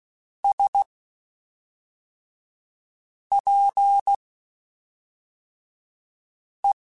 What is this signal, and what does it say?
Morse code, heard as a keyed tone, about 16 wpm, spaced as for 3 wpm, 800 Hz. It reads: SPE
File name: Spaced Characters / Spaced Characters-mp3s / Weather5 12 sp @5